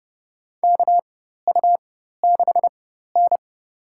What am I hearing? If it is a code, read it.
Morse code, heard as a keyed tone, about 32 wpm, spaced as for 14 wpm, 700 Hz. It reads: KU6N
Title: USA Call Signs - 4 Characters - 4x Character Spacing - 30wpm